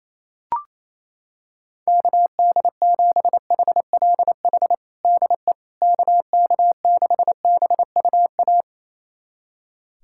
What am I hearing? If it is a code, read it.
Morse code, heard as a keyed tone, about 28 wpm, 700 Hz. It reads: KD7HLH DE KK6BUA
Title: USA Callsign DE Callsign - 6 Characters - 28wpm